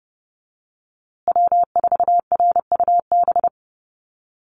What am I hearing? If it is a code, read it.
Morse code, heard as a keyed tone, about 30 wpm, 700 Hz. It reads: W4RUB
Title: USA Call Signs Repeated 2x 30wpm